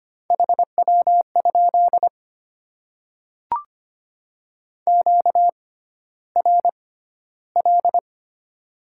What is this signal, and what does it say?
Morse code, heard as a keyed tone, about 25 wpm, spaced as for 13 wpm, 700 Hz. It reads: HW? QRL
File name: Top 20 QSO Elements - 6x Character Spacing - 25wpm